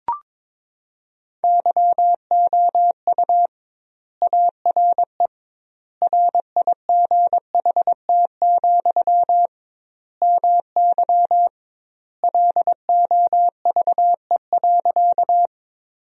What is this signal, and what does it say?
Morse code, heard as a keyed tone, about 22 wpm, 700 Hz. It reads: YOU ARE RIGHT, MY LOVE.
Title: 2x Word-Spacing - Sentences from Top 400 Words - 22wpm